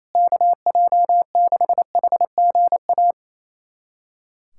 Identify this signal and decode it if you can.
Morse code, heard as a keyed tone, about 28 wpm, 700 Hz. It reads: KJ6HGA
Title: USA Callsign DE Callsign - 6 Characters - 28wpm